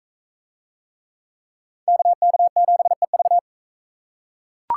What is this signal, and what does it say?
Morse code, heard as a keyed tone, about 41 wpm, 700 Hz. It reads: KK7EV